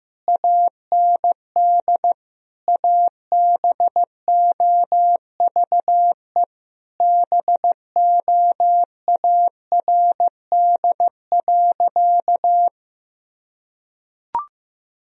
Morse code, heard as a keyed tone, about 15 wpm, 700 Hz. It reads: AND ABOVE BOARD.
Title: Sentences from Top 1000 Words - Review Part 4 - 15wpm